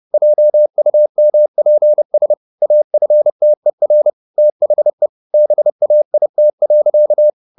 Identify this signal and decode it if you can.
Morse code, heard as a keyed tone, about 30 wpm, 600 Hz. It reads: JUMPS AFTER THE BAIT.
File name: the old man and the sea / the old man and the sea 1 30wpm